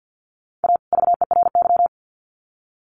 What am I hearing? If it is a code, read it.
Morse code, heard as a keyed tone, about 50 wpm, 700 Hz. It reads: A VERY
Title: Sets of 2 Words 50wpm